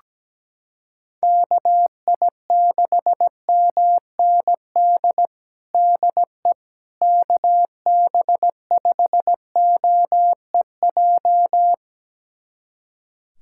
Morse code, heard as a keyed tone, about 17 wpm, 700 Hz. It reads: KI6MND DE KB5OEJ
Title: USA Callsign DE Callsign 17wpm